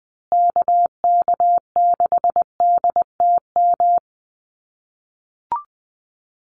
Morse code, heard as a keyed tone, about 20 wpm, 700 Hz. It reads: KK6DTM
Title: USA Call Signs Repeated 3x 20wpm